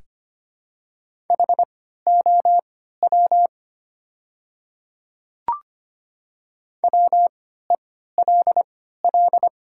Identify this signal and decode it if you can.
Morse code, heard as a keyed tone, about 25 wpm, spaced as for 14 wpm, 700 Hz. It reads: HOW WELL
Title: 3x Character Spacing - Top 200 Words - 25wpm